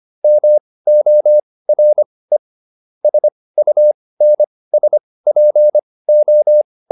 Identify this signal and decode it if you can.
Morse code, heard as a keyed tone, about 25 wpm, spaced as for 18 wpm, 600 Hz. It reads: MORE SUNSPO
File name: SS211 18